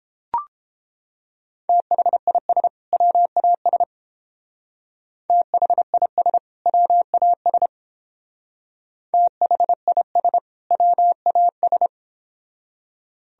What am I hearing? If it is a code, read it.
Morse code, heard as a keyed tone, about 29 wpm, 700 Hz. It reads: THIS WAS THIS WAS THIS WAS